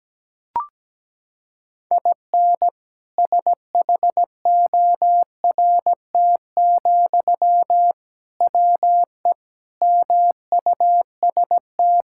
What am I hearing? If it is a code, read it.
Morse code, heard as a keyed tone, about 17 wpm, 700 Hz. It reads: IN SHORT, WE MUST
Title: Sentences from Top 1600 Words 17wpm